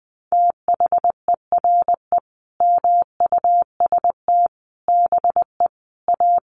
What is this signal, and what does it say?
Morse code, heard as a keyed tone, about 20 wpm, 700 Hz. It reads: THERE MUST BE A
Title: Sentences from Top 1400 Words 20wpm